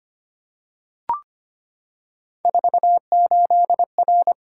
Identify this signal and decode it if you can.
Morse code, heard as a keyed tone, about 25 wpm, 700 Hz. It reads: 48R
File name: Three Letter-Number Combinations - 25wpm